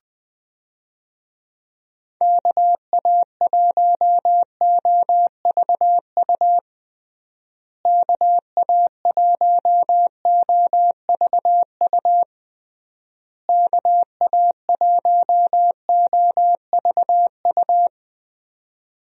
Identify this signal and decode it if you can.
Morse code, heard as a keyed tone, about 19 wpm, 700 Hz. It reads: KA1OVU KA1OVU KA1OVU
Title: USA Call Signs - 6 Characters - Repeated 3x - 20wpm